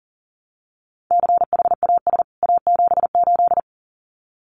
Morse code, heard as a keyed tone, about 40 wpm, 700 Hz. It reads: CHAS A78
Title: CWT Exchange 40wpm